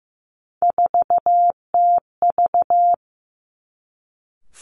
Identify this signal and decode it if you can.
Morse code, heard as a keyed tone, about 15 wpm, 700 Hz. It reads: 4TV
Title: Three Challenging Letter-Number Combinations 15wpm